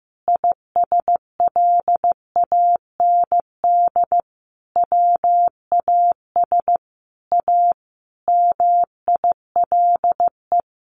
Morse code, heard as a keyed tone, about 15 wpm, 700 Hz. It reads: ISLAND WAS A MILE
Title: Sentences from Top 500 Words 15wpm